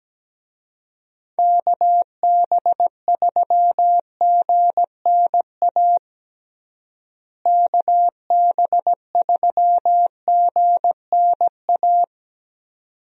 Morse code, heard as a keyed tone, about 17 wpm, 700 Hz. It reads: KB3GNA KB3GNA
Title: USA Call Signs - 6 Characters - Repeated 2x - 17wpm